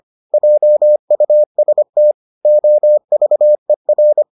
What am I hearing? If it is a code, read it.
Morse code, heard as a keyed tone, about 25 wpm, 600 Hz. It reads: JUST OVER